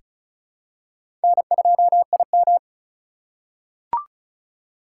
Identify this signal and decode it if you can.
Morse code, heard as a keyed tone, about 36 wpm, 700 Hz. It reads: N2IM
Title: USA Call Signs - 4x Character Spacing - 35wpm